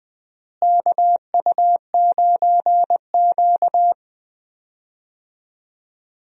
Morse code, heard as a keyed tone, about 20 wpm, 700 Hz. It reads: KU9Q
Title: USA Call Signs - 4 Characters - Repeated 2x - 20wpm